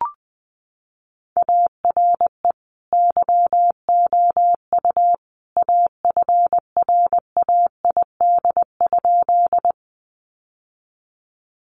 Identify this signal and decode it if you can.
Morse code, heard as a keyed tone, about 20 wpm, 700 Hz. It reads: ARE YOU AFRAID?
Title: Sentences from Top 1500 Words 20wpm